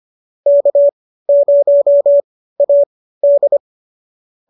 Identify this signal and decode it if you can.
Morse code, heard as a keyed tone, about 25 wpm, spaced as for 15 wpm, 590 Hz. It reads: K0AD
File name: qso213 15